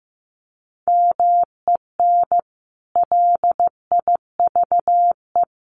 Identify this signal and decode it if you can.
Morse code, heard as a keyed tone, about 15 wpm, 700 Hz. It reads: MEN LIVE
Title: Sentences from Top 1400 Words 15wpm